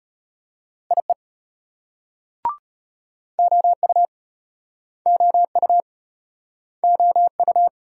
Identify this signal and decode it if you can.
Morse code, heard as a keyed tone, about 33 wpm, 700 Hz. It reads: IE OU OU OU